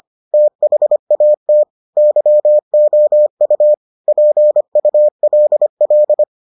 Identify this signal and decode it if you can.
Morse code, heard as a keyed tone, about 25 wpm, 600 Hz. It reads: THAT YOU PULL